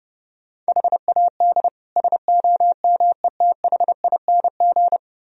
Morse code, heard as a keyed tone, about 39 wpm, 700 Hz. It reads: HAD SOMETHING